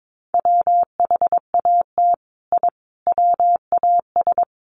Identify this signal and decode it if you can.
Morse code, heard as a keyed tone, about 22 wpm, 700 Hz. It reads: WHAT I WAS
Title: Sets of 5 Words Encore2 22wpm